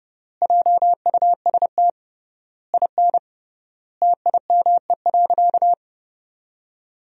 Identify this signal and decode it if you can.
Morse code, heard as a keyed tone, about 31 wpm, 700 Hz. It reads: JUST IN TIME.
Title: 3x Word-Spacing - Sentences from Top 300 Words - 30wpm